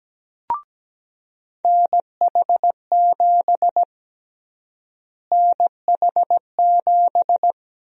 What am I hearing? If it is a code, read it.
Morse code, heard as a keyed tone, about 17 wpm, 700 Hz. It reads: NH7 NH7
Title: USA Call Sign Prefixes - Repeated 2x - 17wpm